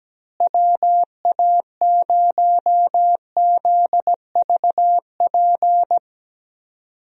Morse code, heard as a keyed tone, about 17 wpm, 700 Hz. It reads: WA0ZVP
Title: USA Call Signs - 6 Characters - Repeated 2x - 17wpm